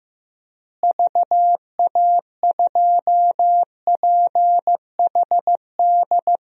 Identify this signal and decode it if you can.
Morse code, heard as a keyed tone, about 16 wpm, 700 Hz. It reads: VA2PHD
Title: DXCC-100 Call Signs - 6 Characters - Repeated 2x - 15wpm